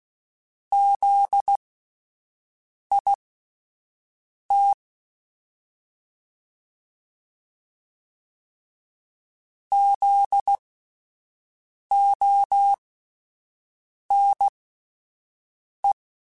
Morse code, heard as a keyed tone, about 16 wpm, spaced as for 5 wpm, 800 Hz. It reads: ZIT ZONE
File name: Spaced Alphas / Spaced Alphas mp3s / Zs sp @10